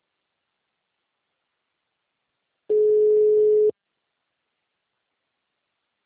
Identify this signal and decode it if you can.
Ringback tone (Europe, 425 Hz).